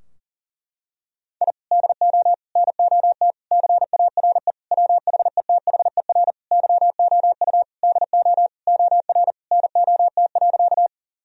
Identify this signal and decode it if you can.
Morse code, heard as a keyed tone, about 40 wpm, 700 Hz. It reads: I DO NOT CARE WHETHER YOU DO OR NOT.
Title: Sentences from Top 1000 Words - Review Part 3 - 40wpm